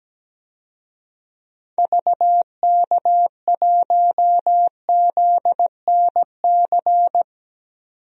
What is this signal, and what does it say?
Morse code, heard as a keyed tone, about 17 wpm, 700 Hz. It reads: VK1ZNC
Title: DXCC-100 Call Signs - 6 Characters - 17wpm